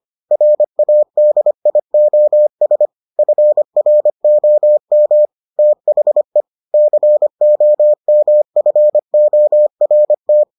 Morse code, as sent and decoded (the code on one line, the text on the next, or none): .-. .- -.. .. --- ... / ..-. .-. --- -- / - .... / -.-. --- -- ..-. --- .-. -
RADIOS FROM THE COMFORT